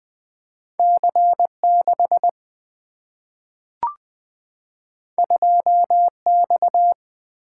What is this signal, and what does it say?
Morse code, heard as a keyed tone, about 20 wpm, 700 Hz. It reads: C 6 2 X
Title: Single-Letter Single-Number 20wpm